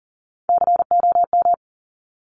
Morse code, heard as a keyed tone, about 38 wpm, 700 Hz. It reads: COM
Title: Top Three Letter Combinations in Words 40wpm